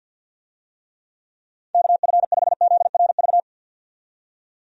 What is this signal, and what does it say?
Morse code, heard as a keyed tone, about 52 wpm, 700 Hz. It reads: KF5ZRV
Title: USA Call Signs - 6 Characters - 50wpm